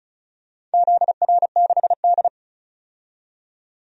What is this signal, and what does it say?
Morse code, heard as a keyed tone, about 36 wpm, 700 Hz. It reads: ZR6D
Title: DXCC-100 Call Signs - 4 Characters - 35wpm